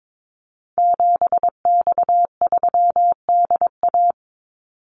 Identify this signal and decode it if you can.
Morse code, heard as a keyed tone, about 22 wpm, 700 Hz. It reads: 7X3DA
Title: Call Signs Repeated 3x - 22wpm